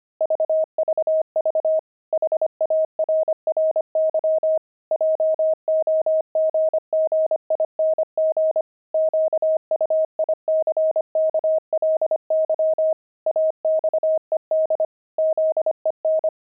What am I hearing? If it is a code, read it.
Morse code, heard as a keyed tone, about 25 wpm, 630 Hz. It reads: VVV HARRY JOGGING QUICKLY AXED ZEN
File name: pangram-fs-25-cs-25-monks